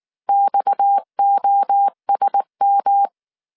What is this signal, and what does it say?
Morse code, heard as a keyed tone, about 19 wpm, 800 Hz. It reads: XOSM
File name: CW / DET / 1 6 1500C S C 18wpm 2356